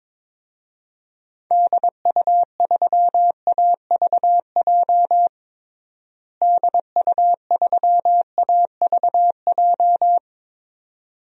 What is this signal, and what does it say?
Morse code, heard as a keyed tone, about 22 wpm, 700 Hz. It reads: DU3AVJ DU3AVJ
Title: DXCC-100 Call Signs - 6 Characters - Repeated 2x - 22wpm